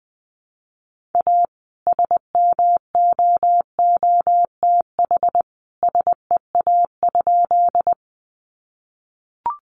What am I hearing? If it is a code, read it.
Morse code, heard as a keyed tone, about 20 wpm, 700 Hz. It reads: A SMOOTH SEA?
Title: Sentences from Top 1400 Words 20wpm